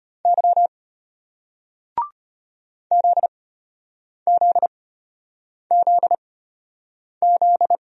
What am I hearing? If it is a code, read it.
Morse code, heard as a keyed tone, about 34 wpm, spaced as for 8 wpm, 700 Hz. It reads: Y ZZZZ